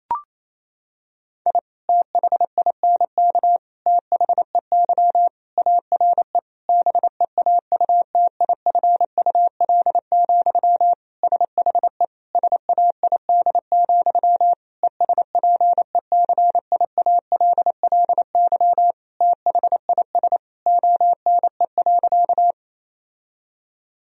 Morse code, heard as a keyed tone, about 27 wpm, 700 Hz. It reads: I THINK THEY ARE BEAUTIFUL, SHE SAID, ESPECIALLY THIS ONE.
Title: Sentences from Top 1000 Words - Review Part 4 - 28wpm